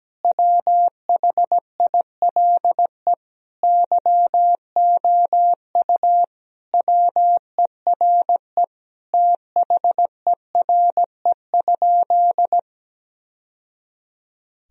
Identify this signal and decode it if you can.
Morse code, heard as a keyed tone, about 17 wpm, 700 Hz. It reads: WHILE YOU WERE THERE?
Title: Sentences from Top 1000 Words - Review Part 1 - 17wpm